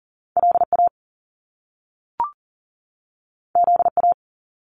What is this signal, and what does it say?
Morse code, heard as a keyed tone, about 38 wpm, 700 Hz. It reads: LA ZA
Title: Top Two Letter Combinations in Words 40wpm